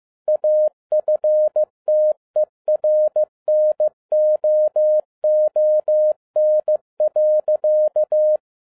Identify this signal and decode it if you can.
Morse code, heard as a keyed tone, about 15 wpm, 610 Hz. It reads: AFTERNOON.
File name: 2011-07-mp3 / n47 JOHN K3TN 15wpm